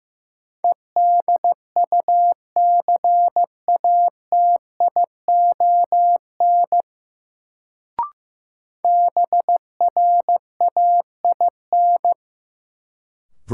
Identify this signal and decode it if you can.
Morse code, heard as a keyed tone, about 15 wpm, 700 Hz. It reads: EDUCATION BRAIN
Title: Sentences from Top 1100 Words - New Word Drill Alt - 15wpm